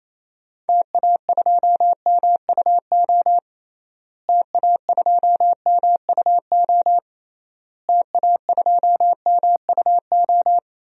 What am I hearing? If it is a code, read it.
Morse code, heard as a keyed tone, about 28 wpm, 700 Hz. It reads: TA2MUO TA2MUO TA2MUO
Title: DXCC-100 Call Signs - 6 Characters - Repeated 3x - 28wpm